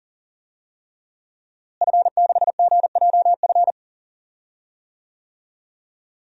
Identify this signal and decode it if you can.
Morse code, heard as a keyed tone, about 40 wpm, 700 Hz. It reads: F6GJF
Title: DXCC-100 Call Signs - 5 Characters - Repeated 3x - 40wpm